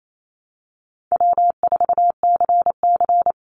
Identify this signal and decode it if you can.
Morse code, heard as a keyed tone, about 28 wpm, 700 Hz. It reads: W4CC
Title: Call Signs - 28wpm